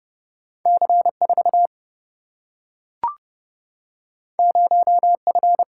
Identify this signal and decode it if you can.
Morse code, heard as a keyed tone, about 30 wpm, 700 Hz. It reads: C4 0F
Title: Single-Letter Single-Number 30wpm